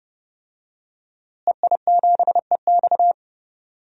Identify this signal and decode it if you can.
Morse code, heard as a keyed tone, about 30 wpm, 700 Hz. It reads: EI7EX
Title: DXCC-100 Call Signs - 5 Characters - Repeated 3x - 30wpm